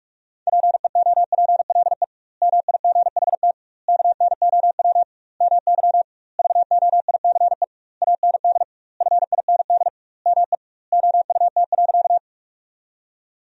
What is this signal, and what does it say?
Morse code, heard as a keyed tone, about 46 wpm, 700 Hz. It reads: PEOPLE MIGHT KNOW MY VOICE AND FIND ME OUT.
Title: 2x Word-Spacing - Sentences from Top 500 Words - 45wpm